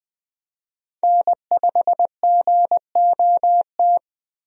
Morse code, heard as a keyed tone, about 20 wpm, 700 Hz. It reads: N5GOT
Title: USA Call Signs - 5 Characters - Repeated 2x - 20wpm